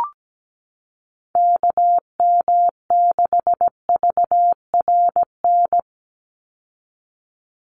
Morse code, heard as a keyed tone, about 17 wpm, 700 Hz. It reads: KM6VRN
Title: USA Call Signs 17wpm